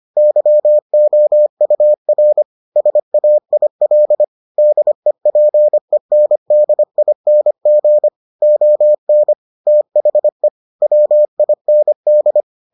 Morse code, heard as a keyed tone, about 25 wpm, 600 Hz. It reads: YOUR SAIL DEPENDING ON THE WIND